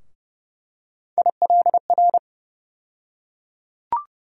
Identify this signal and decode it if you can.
Morse code, heard as a keyed tone, about 30 wpm, 700 Hz. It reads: ILR
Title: USA Call Sign Postfixes - 3 Letters 30wpm